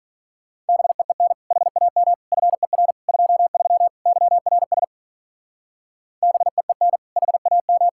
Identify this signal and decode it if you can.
Morse code, heard as a keyed tone, about 44 wpm, 700 Hz. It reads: BEEN HAM FER 23 YRS BEEN HAM
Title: Speed-Racing Common QSO Phrases Encore - 35wpm